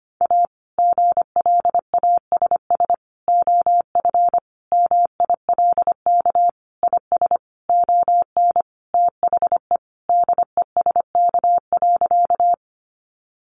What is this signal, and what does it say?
Morse code, heard as a keyed tone, about 25 wpm, 700 Hz. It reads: A GLASS OF MILK IS ON THE DESK.
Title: Additional Easy Phrases 25wpm